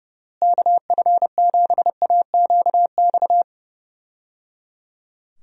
Morse code, heard as a keyed tone, about 30 wpm, 700 Hz. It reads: KF7AQX
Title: USA Call Signs - 6 Characters - Repeated 3x - 30wpm